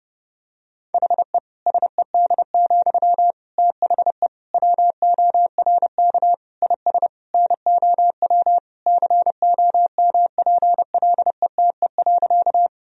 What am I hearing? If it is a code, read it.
Morse code, heard as a keyed tone, about 32 wpm, 700 Hz. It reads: HE SED, THE WORK IS NOW COMPLETE.
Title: Sentences from Top 400 Words and Abbreviations 30wpm